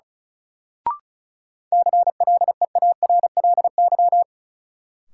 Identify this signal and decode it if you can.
Morse code, heard as a keyed tone, about 36 wpm, 700 Hz. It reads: CLEARLY
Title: Sentences from Top 1300 Words - New Word Drill Alt - 35wpm